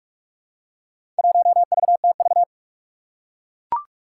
Morse code, heard as a keyed tone, about 44 wpm, 700 Hz. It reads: 1VTV
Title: Four Challenging Letter-Number Combinations 45wpm